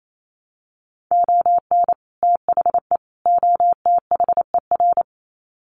Morse code, heard as a keyed tone, about 28 wpm, 700 Hz. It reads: ON THE OTHER
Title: Sets of 3 Words 28wpm